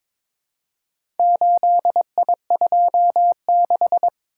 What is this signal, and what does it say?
Morse code, heard as a keyed tone, about 22 wpm, 700 Hz. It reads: 8I26
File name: practice / 22wpm / Four Letter-Number Combinations - 22wpm